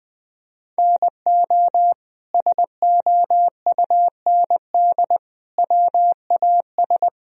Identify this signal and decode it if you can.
Morse code, heard as a keyed tone, about 20 wpm, 700 Hz. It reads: NO SOUND WAS
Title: Sentences from Top 1000 Words - Review Part 1 - 20wpm